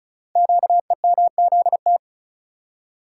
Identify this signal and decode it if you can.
Morse code, heard as a keyed tone, about 36 wpm, 700 Hz. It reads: QEMZT
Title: Code Groups - 5 Letters - 35wpm